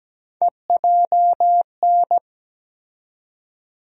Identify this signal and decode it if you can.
Morse code, heard as a keyed tone, about 17 wpm, 700 Hz. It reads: EJN
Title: USA Call Sign Postfixes - 3 Letters 17wpm